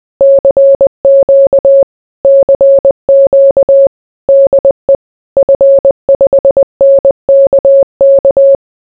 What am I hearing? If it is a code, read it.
Morse code, heard as a keyed tone, about 20 wpm, 550 Hz. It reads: CQ CQ DE F5NKK
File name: CQCQCQDEF5NKK